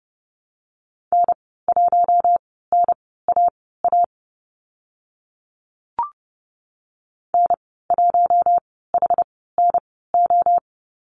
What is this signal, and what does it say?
Morse code, heard as a keyed tone, about 30 wpm, spaced as for 17 wpm, 700 Hz. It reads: N1NAA N1HNO
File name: Lesson 15g - Number 5 - Call Signs 30wpm